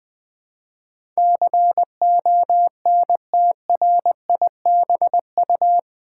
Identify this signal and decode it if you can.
Morse code, heard as a keyed tone, about 21 wpm, 700 Hz. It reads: CONTRIBU
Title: Build A Word - 4 Letters At A Time - 12-Letter Words - 20wpm